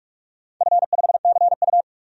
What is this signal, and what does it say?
Morse code, heard as a keyed tone, about 46 wpm, 700 Hz. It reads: F5CU
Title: DXCC-100 Call Signs - 4 Characters - 45wpm